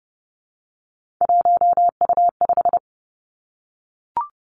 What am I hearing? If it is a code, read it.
Morse code, heard as a keyed tone, about 30 wpm, 700 Hz. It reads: 1U5
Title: Three Challenging Letter-Number Combinations 30wpm